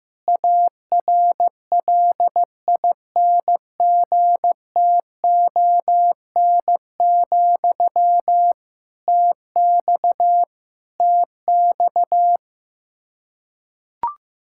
Morse code, heard as a keyed tone, about 15 wpm, 700 Hz. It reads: ARLINGTON, TX TX
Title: K1USN SST QSO Practice 15wpm